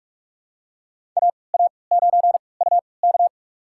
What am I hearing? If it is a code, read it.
Morse code, heard as a keyed tone, about 46 wpm, spaced as for 26 wpm, 700 Hz. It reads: AA9UK
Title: USA Call Signs - 3x Character Spacing - 45wpm